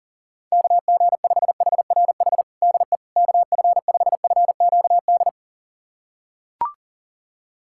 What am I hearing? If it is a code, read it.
Morse code, heard as a keyed tone, about 40 wpm, 690 Hz. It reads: KG5HRH DE KF5FQD
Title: USA Callsign DE Callsign - 6 Characters - 40wpm